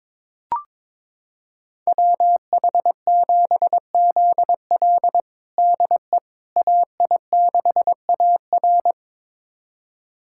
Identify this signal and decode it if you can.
Morse code, heard as a keyed tone, about 22 wpm, 700 Hz. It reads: WH7ZL DE AI6AR
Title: USA Callsign DE Callsign - 5 Characters - 22wpm